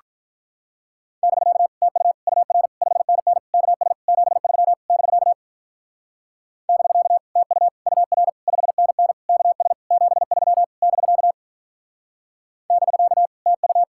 Morse code, heard as a keyed tone, about 47 wpm, 700 Hz. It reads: <BK> TU UR 5NN KS 73 <BK> <BK> TU UR 5NN KS 73 <BK> <BK> TU